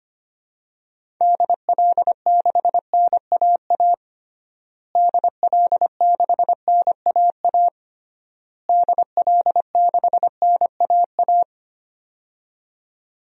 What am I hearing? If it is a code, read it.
Morse code, heard as a keyed tone, about 25 wpm, 700 Hz. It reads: DL6NAA DL6NAA DL6NAA
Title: DXCC-100 Call Signs - 6 Characters - Repeated 3x - 25wpm